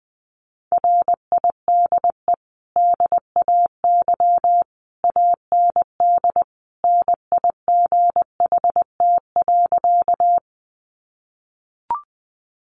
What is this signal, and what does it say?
Morse code, heard as a keyed tone, about 20 wpm, 700 Hz. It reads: RIDE DAY AND NIGHT.
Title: Sentences from Top 1200 Words 20wpm